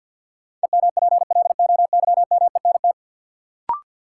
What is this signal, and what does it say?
Morse code, heard as a keyed tone, about 53 wpm, 700 Hz. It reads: EMPLOYMENT
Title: Sentences from Top 1500 Words - New Word Drill - 50wpm